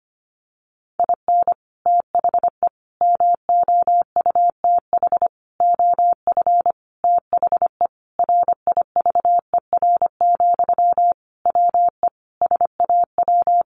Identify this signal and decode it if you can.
Morse code, heard as a keyed tone, about 25 wpm, 700 Hz. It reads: IN THE MOUTH OF THE RIVER, WE SAW
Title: Sentences from Top 1500 Words 25wpm